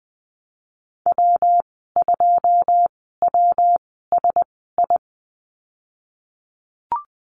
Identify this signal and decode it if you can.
Morse code, heard as a keyed tone, about 20 wpm, spaced as for 15 wpm, 700 Hz. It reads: W2WSI
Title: 173 Lesson 28g - Number 9 - Call Signs 20wpm